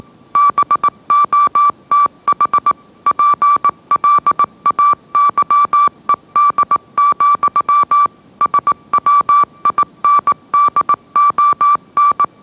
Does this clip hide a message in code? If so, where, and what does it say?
Yes: Morse code from 0.4 to 12.2 s, heard as a keyed tone, about 18 wpm, 1160 Hz. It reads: BOTH PLAYED, SWINDON